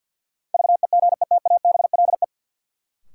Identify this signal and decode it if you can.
Morse code, heard as a keyed tone, about 53 wpm, 700 Hz. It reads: VEGETABLE
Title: Sentences from Top 1300 Words - New Word Drill - 50wpm